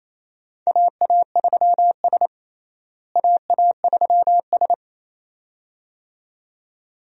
Morse code, heard as a keyed tone, about 28 wpm, 700 Hz. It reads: AA3S AA3S
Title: USA Call Signs - 4 Characters - Repeated 2x - 28wpm